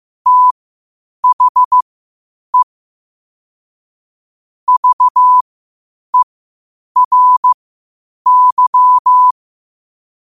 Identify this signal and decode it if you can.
Morse code, heard as a keyed tone, about 15 wpm, spaced as for 8 wpm, 1000 Hz. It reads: THE VERY
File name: qotdfiles / 8wpm